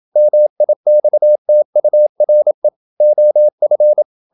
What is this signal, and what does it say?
Morse code, heard as a keyed tone, about 27 wpm, 600 Hz. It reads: MIXTURE OF
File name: Never 0027wpm